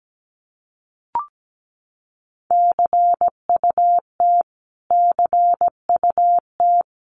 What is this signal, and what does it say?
Morse code, heard as a keyed tone, about 17 wpm, 700 Hz. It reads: CUT CUT
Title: Top 500 Words Repeated 17wpm